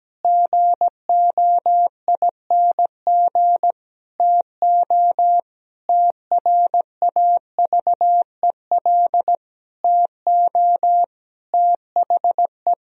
Morse code, heard as a keyed tone, about 17 wpm, 700 Hz. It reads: GOING TO TRAVEL TO THE
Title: Sentences from Top 1000 Words - Review Part 3 - 17wpm